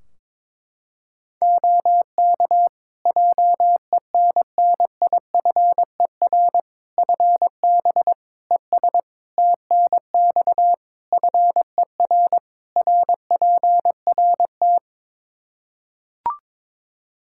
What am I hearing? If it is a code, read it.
Morse code, heard as a keyed tone, about 22 wpm, 700 Hz. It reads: OK JENNIFER FB ES TNX FER RPRT
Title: Practice QSO - Exchange 2 - Info - 22wpm